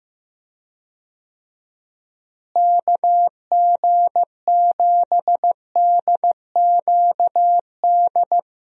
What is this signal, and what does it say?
Morse code, heard as a keyed tone, about 15 wpm, 700 Hz. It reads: KG7DQD
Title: USA Call Signs - 6 Characters - Repeated 2x - 15wpm